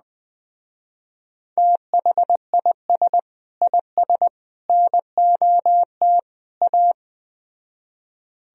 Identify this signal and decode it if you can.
Morse code, heard as a keyed tone, about 20 wpm, 700 Hz. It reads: THIS IS NOT A